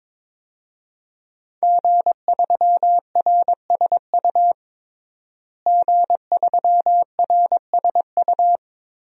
Morse code, heard as a keyed tone, about 22 wpm, 700 Hz. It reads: G3RSU G3RSU
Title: DXCC-100 Call Signs - 5 Characters - Repeated 2x - 22wpm